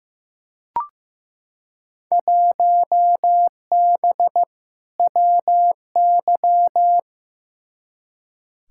Morse code, heard as a keyed tone, about 15 wpm, 700 Hz. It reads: 1B WY
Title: Field Day Exchanges Encore - 15wpm